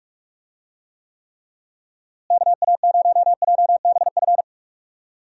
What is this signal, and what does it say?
Morse code, heard as a keyed tone, about 46 wpm, 700 Hz. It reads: KA0JBF
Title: USA Call Signs - 6 Characters - Repeated 3x - 45wpm